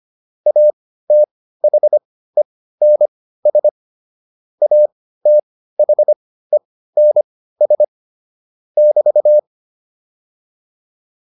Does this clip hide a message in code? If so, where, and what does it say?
Yes: Morse code from 0.5 to 9.4 s, heard as a keyed tone, about 25 wpm, spaced as for 15 wpm, 600 Hz. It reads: ATHENS ATHENS <BT>